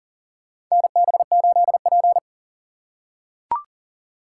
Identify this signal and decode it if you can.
Morse code, heard as a keyed tone, about 40 wpm, 700 Hz. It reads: NB8P
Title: USA Call Signs - 2x Character Spacing - 40wpm